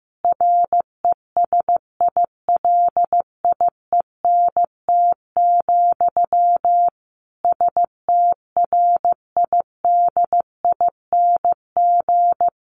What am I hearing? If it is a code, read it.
Morse code, heard as a keyed tone, about 14 wpm, 700 Hz. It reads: RESILIENT, STRIDING
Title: A Sound Of Thunder 15wpm